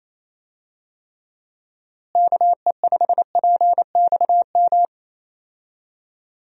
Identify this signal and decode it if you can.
Morse code, heard as a keyed tone, about 28 wpm, 700 Hz. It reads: KE5PXM
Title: USA Call Signs - 6 Characters - Repeated 3x - 28wpm